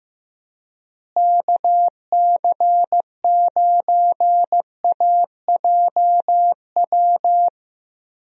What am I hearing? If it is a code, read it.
Morse code, heard as a keyed tone, about 15 wpm, 700 Hz. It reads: KC9AJW
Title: USA Call Signs - 6 Characters - Repeated 3x - 15wpm